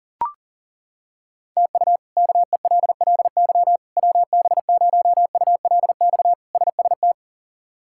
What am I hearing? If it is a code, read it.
Morse code, heard as a keyed tone, about 40 wpm, 700 Hz. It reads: TU KELLY WB0ULX SST
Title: K1USN SST QSO Practice - 40wpm